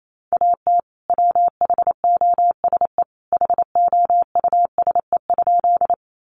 Morse code, heard as a keyed tone, about 28 wpm, 700 Hz. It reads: AT WHOSE HOUSE?
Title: Sentences from Top 1200 Words 28wpm